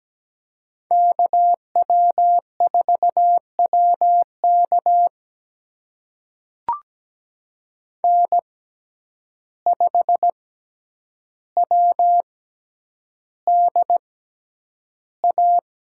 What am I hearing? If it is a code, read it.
Morse code, heard as a keyed tone, about 17 wpm, spaced as for 9 wpm, 700 Hz. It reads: KW4WK N5WDA
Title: USA Call Signs - 6x Character Spacing - 17wpm